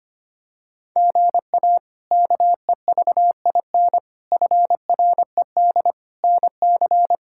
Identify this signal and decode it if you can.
Morse code, heard as a keyed tone, about 25 wpm, 700 Hz. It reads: GA KEVIN FRED NC
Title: K1USN SST QSO Practice - 25wpm